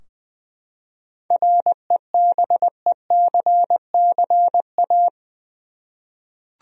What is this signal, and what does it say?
Morse code, heard as a keyed tone, about 20 wpm, 700 Hz. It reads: REBECCA
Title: USA Common Names - 20wpm